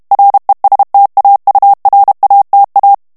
Morse code, heard as a keyed tone, about 32 wpm, 800 Hz. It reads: RESTAURATA
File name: morse mp3 / ITALIEN texte clair 1 vitesse 30 mots minute